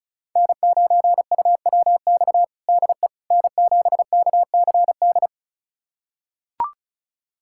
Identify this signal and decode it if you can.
Morse code, heard as a keyed tone, about 34 wpm, 700 Hz. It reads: N9UWX DE N7KCD
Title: USA Callsign DE Callsign - 5 Characters - 35wpm